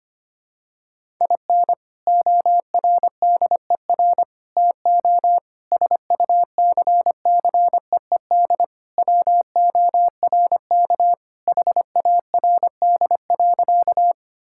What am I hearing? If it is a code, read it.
Morse code, heard as a keyed tone, about 25 wpm, 700 Hz. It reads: IN ORDER TO SUCCEED WORK HARD.
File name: practice / 25wpm / Sentences with 3-Word Phrases - 25wpm